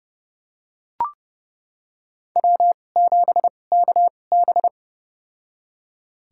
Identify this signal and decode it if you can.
Morse code, heard as a keyed tone, about 30 wpm, spaced as for 22 wpm, 700 Hz. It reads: W7KB
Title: USA Call Signs - 2x Character Spacing - 30wpm